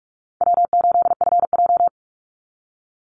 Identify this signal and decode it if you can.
Morse code, heard as a keyed tone, about 44 wpm, 700 Hz. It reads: W8FJ
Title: Easy Call Signs - 45wpm